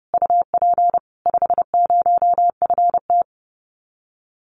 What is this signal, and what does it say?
Morse code, heard as a keyed tone, about 30 wpm, 700 Hz. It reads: UP 50FT
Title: Common QSO Phrases Encore 30wpm